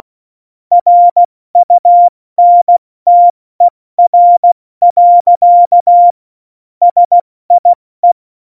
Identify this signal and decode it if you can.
Morse code, heard as a keyed tone, about 16 wpm, 700 Hz. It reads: RUNTER. SIE
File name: Morsekurs / Witze / Witz 29. Dezember 2020, 70 F80